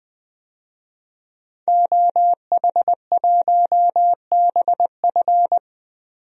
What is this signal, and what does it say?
Morse code, heard as a keyed tone, about 21 wpm, 700 Hz. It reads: OH1BF
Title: DXCC-100 Call Signs - 5 Characters - 20wpm